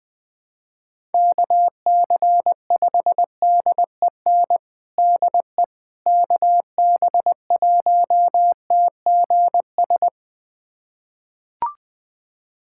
Morse code, heard as a keyed tone, about 20 wpm, 700 Hz. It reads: KC5DEN DE KB1TGS
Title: USA Callsign DE Callsign - 6 Characters - 20wpm